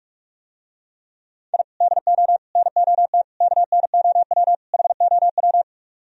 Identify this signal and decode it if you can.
Morse code, heard as a keyed tone, about 48 wpm, 700 Hz. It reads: I DO NOT KNOW HOW